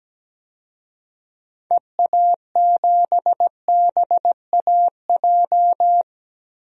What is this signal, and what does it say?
Morse code, heard as a keyed tone, about 17 wpm, 700 Hz. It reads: EA7BAJ
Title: DXCC-100 Call Signs - 6 Characters - Repeated 3x - 17wpm